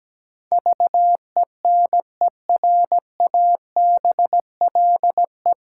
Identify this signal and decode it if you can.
Morse code, heard as a keyed tone, about 17 wpm, 700 Hz. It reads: VENERABLE
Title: Pangram Challenge v2 - 17wpm